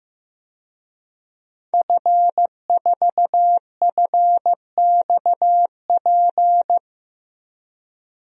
Morse code, heard as a keyed tone, about 15 wpm, 700 Hz. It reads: F4FXP
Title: DXCC-100 Call Signs - 5 Characters - 15wpm